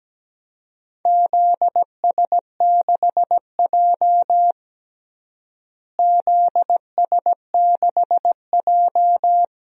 Morse code, heard as a keyed tone, about 17 wpm, 700 Hz. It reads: ZS6J ZS6J
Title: DXCC-100 Call Signs - 4 Characters - Repeated 3x - 17wpm